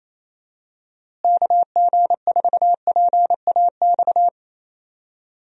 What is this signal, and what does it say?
Morse code, heard as a keyed tone, about 28 wpm, 700 Hz. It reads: KG4PAX
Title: USA Call Signs - 6 Characters - Repeated 2x - 28wpm